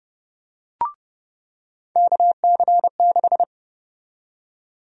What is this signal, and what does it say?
Morse code, heard as a keyed tone, about 32 wpm, 700 Hz. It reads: KC6
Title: USA Call Sign Prefixes - 30wpm